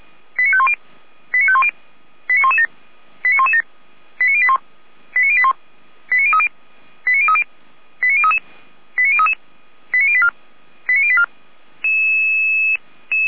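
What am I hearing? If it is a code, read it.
Five-tone selective call (ZVEI), heard as quick runs of five tones, tones of about 70 ms. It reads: 89510 89510 89117 89117 89081 89081 89030 89030 89033 89033 89084 89084
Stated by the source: ZVEI five-tone selective call 89092 received by SDR.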